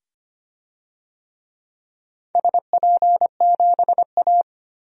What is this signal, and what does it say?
Morse code, heard as a keyed tone, about 25 wpm, 700 Hz. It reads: SP7A
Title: DXCC-100 Call Signs - 4 Characters - 25wpm